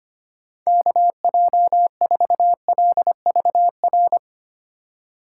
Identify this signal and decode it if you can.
Morse code, heard as a keyed tone, about 25 wpm, 700 Hz. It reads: KJ4LVR
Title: USA Call Signs - 6 Characters - 3x Character Spacing - 25wpm